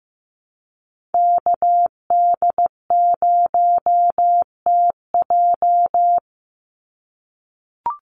Morse code, heard as a keyed tone, about 15 wpm, 700 Hz. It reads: KD0TJ
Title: USA Call Signs 15wpm